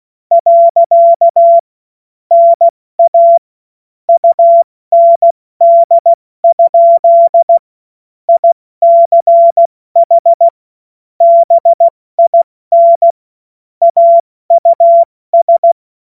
Morse code, heard as a keyed tone, about 16 wpm, 680 Hz. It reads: . NA UND? ICH BIN AUS
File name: Morsekurs / Witze / Witz 18. Januar 2022, 70 F80